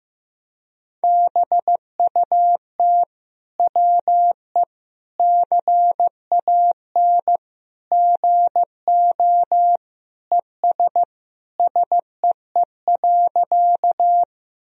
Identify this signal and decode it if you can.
Morse code, heard as a keyed tone, about 15 wpm, 700 Hz. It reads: BUT WE CAN GO ES SEE.
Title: Sentences from Top 100 Words and Abbreviations 15wpm